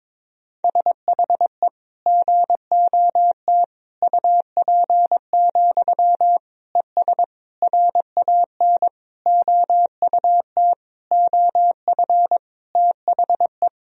Morse code, heard as a keyed tone, about 22 wpm, 700 Hz. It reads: SHE GOT UP, ES RAN OUT OF THE
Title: Sentences from Top 500 Words and Abbreviations 22wpm